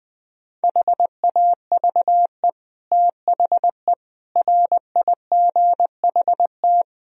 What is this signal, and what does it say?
Morse code, heard as a keyed tone, about 20 wpm, 700 Hz. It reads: HAVE THE RIGHT
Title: Speed-Racing - Sets of 4 Words 20wpm